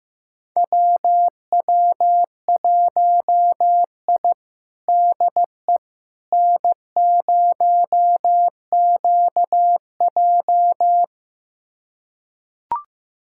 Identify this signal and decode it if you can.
Morse code, heard as a keyed tone, about 15 wpm, 700 Hz. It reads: WW1I DE N0QJ
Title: USA Callsign DE Callsign - 4 Characters - 15wpm